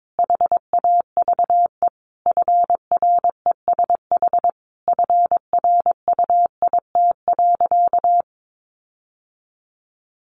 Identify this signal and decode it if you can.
Morse code, heard as a keyed tone, about 22 wpm, 700 Hz. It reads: HAVE FRESH FRUIT.
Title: Sentences from Top 1300 Words 22wpm